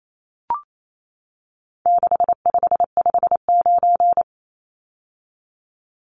Morse code, heard as a keyed tone, about 28 wpm, 700 Hz. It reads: 6559
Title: Numbers - 4-Digit 28wpm